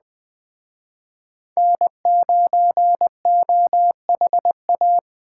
Morse code, heard as a keyed tone, about 20 wpm, 700 Hz. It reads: N9OHA
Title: Speed Racing - USA 1x3 Call Signs 17wpm